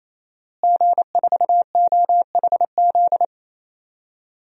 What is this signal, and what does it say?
Morse code, heard as a keyed tone, about 28 wpm, 700 Hz. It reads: G4OHZ
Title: DXCC-100 Call Signs - 5 Characters - Repeated 3x - 28wpm